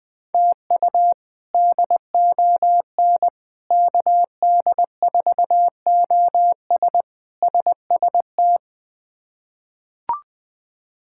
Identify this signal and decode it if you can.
Morse code, heard as a keyed tone, about 20 wpm, 700 Hz. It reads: TU DON KD4OS SST
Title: K1USN SST QSO Practice - 20wpm